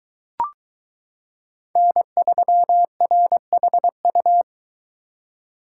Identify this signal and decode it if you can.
Morse code, heard as a keyed tone, about 22 wpm, 700 Hz. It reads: N3RHU